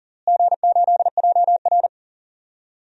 Morse code, heard as a keyed tone, about 40 wpm, 700 Hz. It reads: G8JR
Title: DXCC-100 Call Signs - 4 Characters - Repeated 2x - 40wpm